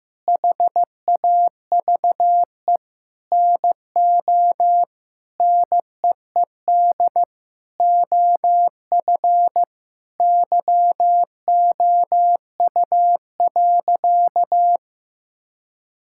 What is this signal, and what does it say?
Morse code, heard as a keyed tone, about 16 wpm, 700 Hz. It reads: HAVE NO NEED OF YOU.
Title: Sentences from Top 1000 Words - Review Part 5 - 15wpm